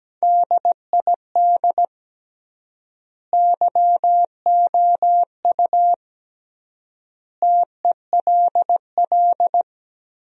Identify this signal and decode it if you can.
Morse code, heard as a keyed tone, about 17 wpm, 700 Hz. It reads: DID YOU TELL
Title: 3x Word-Spacing - Sentences from Top 200 Words - 17wpm